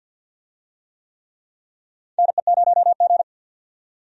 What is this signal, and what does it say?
Morse code, heard as a keyed tone, about 48 wpm, 700 Hz. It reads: NE0G
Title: USA Call Signs - 4 Characters - 50wpm